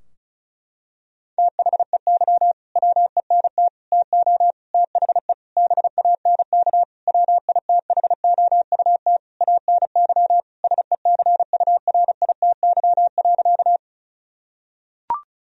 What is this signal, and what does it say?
Morse code, heard as a keyed tone, about 36 wpm, 700 Hz. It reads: THEY WENT TO THE BANK WITHOUT ANY SECURITY.
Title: Sentences from Top 1000 Words - Review Part 4 - 35wpm